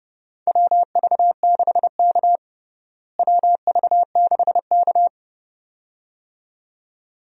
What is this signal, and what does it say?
Morse code, heard as a keyed tone, about 32 wpm, 700 Hz. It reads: WV6K WV6K
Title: USA Call Signs - 4 Characters - Repeated 2x - 30wpm